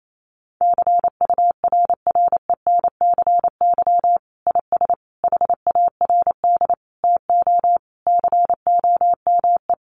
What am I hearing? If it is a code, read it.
Morse code, heard as a keyed tone, about 28 wpm, 700 Hz. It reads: CURRENCY IS HARD TO COME
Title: Sentences from Top 1500 Words 28wpm